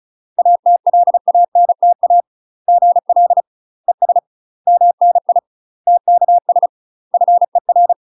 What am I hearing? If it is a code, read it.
Morse code, heard as a keyed tone, about 34 wpm, 700 Hz. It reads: ATLANTA GL ES MNI TKS FER
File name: UK MASTER QSOs TRAINING / 35WPM / UK MASTER QSOs 0015